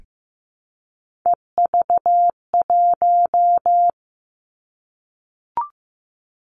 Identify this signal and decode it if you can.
Morse code, heard as a keyed tone, about 15 wpm, 700 Hz. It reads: EV1
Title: Three Challenging Letter-Number Combinations 15wpm